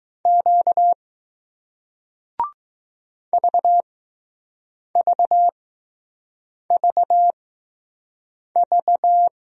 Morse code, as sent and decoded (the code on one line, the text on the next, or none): --.- / ...- ...- ...- ...-
Q VVVV